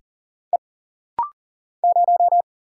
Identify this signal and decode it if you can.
Morse code, heard as a keyed tone, about 40 wpm, 700 Hz. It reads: E 0
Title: Single Letter-Number - Rapid-Fire 40wpm